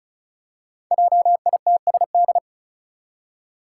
Morse code, heard as a keyed tone, about 34 wpm, 700 Hz. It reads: JITSD
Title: Five Challenging Letter-Number Combinations 35wpm